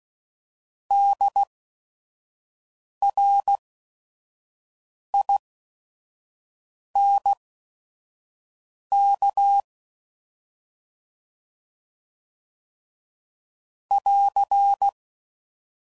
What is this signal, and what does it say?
Morse code, heard as a keyed tone, about 16 wpm, spaced as for 5 wpm, 800 Hz. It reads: DRINK <AR>
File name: Prefixes / By First Letter / Ds